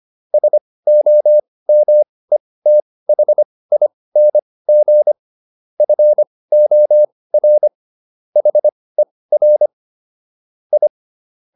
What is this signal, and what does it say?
Morse code, heard as a keyed tone, about 25 wpm, spaced as for 18 wpm, 600 Hz. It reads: SOMETHING FOR HER I